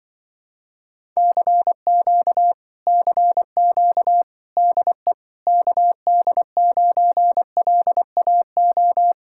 Morse code, heard as a keyed tone, about 24 wpm, 700 Hz. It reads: CQ CQ DE KD9LAO
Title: Speed-Racing Common QSO Phrases - 20wpm